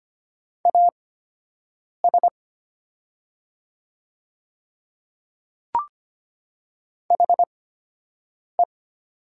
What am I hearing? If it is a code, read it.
Morse code, heard as a keyed tone, about 25 wpm, spaced as for 7 wpm, 700 Hz. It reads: AS HE
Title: 8x Character Spacing - Top 100 Words - 25wpm